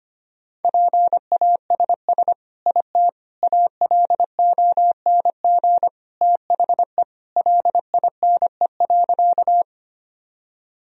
Morse code, heard as a keyed tone, about 25 wpm, 700 Hz. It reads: PASS IT ALONG THE LINE.
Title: Sentences from Top 1000 Words - Review Part 1 - 25wpm